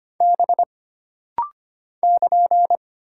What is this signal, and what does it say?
Morse code, heard as a keyed tone, about 25 wpm, 700 Hz. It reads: B <KN>
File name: Single Character (Letter:Number:Punctuation:Prosign) - Rapid-Fire - 25wpm